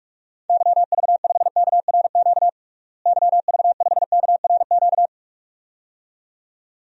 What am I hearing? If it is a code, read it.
Morse code, heard as a keyed tone, about 43 wpm, 700 Hz. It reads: YV5KRQ YV5KRQ
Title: DXCC-100 Call Signs - 6 Characters - Repeated 2x - 45wpm